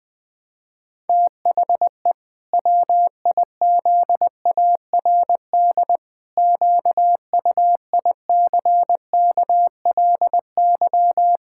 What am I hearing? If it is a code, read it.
Morse code, heard as a keyed tone, about 20 wpm, 700 Hz. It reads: THE WIZARD QUICKLY
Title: Pangram Challenge v2 - 20wpm